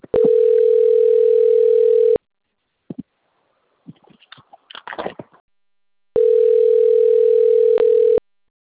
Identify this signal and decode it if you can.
Ringback tone (North America), 440 + 480 Hz.